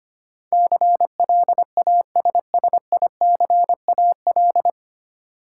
Morse code, heard as a keyed tone, about 25 wpm, 700 Hz. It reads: CLASSICAL
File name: Nine Letter Words - 25wpm